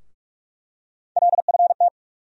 Morse code, heard as a keyed tone, about 46 wpm, 700 Hz. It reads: LFT
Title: USA Call Sign Postfixes - 3 Letters 45wpm